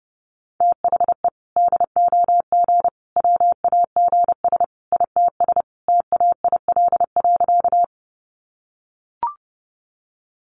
Morse code, heard as a keyed tone, about 30 wpm, 700 Hz. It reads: THE DOG WAGS ITS TAIL.
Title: Easy Phrases 30wpm